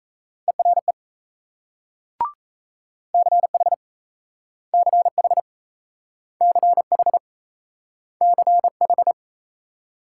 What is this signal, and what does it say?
Morse code, heard as a keyed tone, about 37 wpm, 700 Hz. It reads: ERE CH CH CH CH